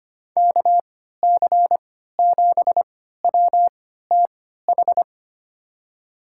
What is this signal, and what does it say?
Morse code, heard as a keyed tone, about 25 wpm, spaced as for 14 wpm, 700 Hz. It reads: KC7WTH
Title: USA Call Signs - 6 Characters - 3x Character Spacing - 25wpm